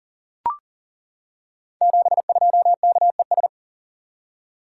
Morse code, heard as a keyed tone, about 42 wpm, 690 Hz. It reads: 72KES
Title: Code Groups - 5 Letter:Numbers - 40wpm